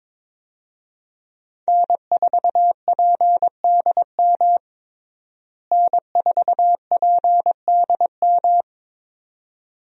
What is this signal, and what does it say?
Morse code, heard as a keyed tone, about 22 wpm, 700 Hz. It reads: N4PDM N4PDM
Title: USA Call Signs - 5 Characters - Repeated 2x - 22wpm